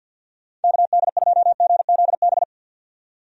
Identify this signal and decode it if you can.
Morse code, heard as a keyed tone, about 52 wpm, 700 Hz. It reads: KD2GZB
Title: USA Call Signs - 6 Characters - 50wpm